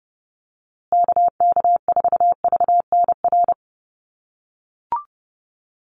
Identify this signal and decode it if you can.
Morse code, heard as a keyed tone, about 30 wpm, 700 Hz. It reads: KK4VNR
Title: USA Call Signs Repeated 2x 30wpm